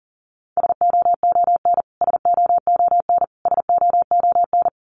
Morse code, heard as a keyed tone, about 40 wpm, 700 Hz. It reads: SOON SOON SOON
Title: Top 500 Words Repeated 40wpm